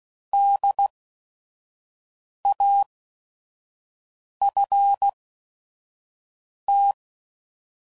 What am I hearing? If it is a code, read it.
Morse code, heard as a keyed tone, about 16 wpm, spaced as for 5 wpm, 800 Hz. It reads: DAFT
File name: Prefixes / By First Letter / Ds